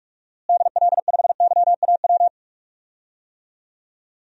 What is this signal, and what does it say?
Morse code, heard as a keyed tone, about 46 wpm, 700 Hz. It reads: DL5YAW
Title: DXCC-100 Call Signs - 6 Characters - Repeated 3x - 45wpm